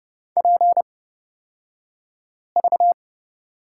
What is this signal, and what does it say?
Morse code, heard as a keyed tone, about 29 wpm, spaced as for 5 wpm, 700 Hz. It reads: PV
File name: Lesson 38d - Letter q - Two Characters 30wpm